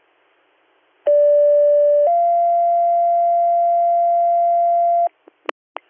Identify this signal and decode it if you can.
Two-tone page, heard as a one-second tone, then a three-second tone, about 580 then 690 Hz. It is heard from 1.1 to 5.1 s.